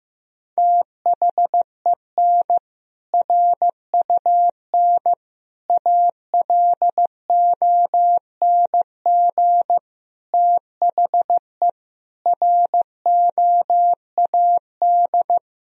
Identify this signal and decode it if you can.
Morse code, heard as a keyed tone, about 15 wpm, 700 Hz. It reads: THEN RUN ALONG THE ROAD
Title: Sentences from Top 1000 Words - Review Part 2 - 15wpm